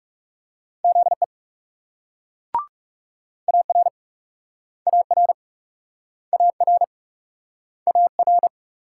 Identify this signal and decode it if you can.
Morse code, heard as a keyed tone, about 40 wpm, 700 Hz. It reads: ZE AR AR AR AR